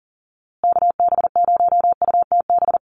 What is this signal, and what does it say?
Morse code, heard as a keyed tone, about 40 wpm, 700 Hz. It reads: KB0UTB
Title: USA Call Signs Repeated 3x 40wpm